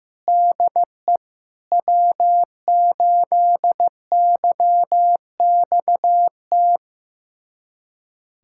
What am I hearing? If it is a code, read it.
Morse code, heard as a keyed tone, about 15 wpm, 700 Hz. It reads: DE W8YXT
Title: USA Callsign DE Callsign - 5 Characters - 15wpm